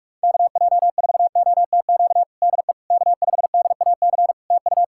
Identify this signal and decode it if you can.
Morse code, heard as a keyed tone, about 46 wpm, 700 Hz. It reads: KJ4OTQ DE K5DAC TU
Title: Practice QSO - Exchange 4 - Ending - 45wpm